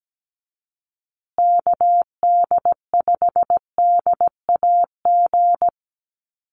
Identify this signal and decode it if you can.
Morse code, heard as a keyed tone, about 17 wpm, 700 Hz. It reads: KD5DAG
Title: USA Call Signs Repeated 2x 17wpm